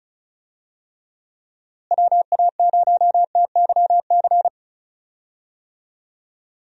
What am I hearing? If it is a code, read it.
Morse code, heard as a keyed tone, about 36 wpm, 700 Hz. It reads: WA0TYC
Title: USA Call Signs - 6 Characters - Repeated 3x - 35wpm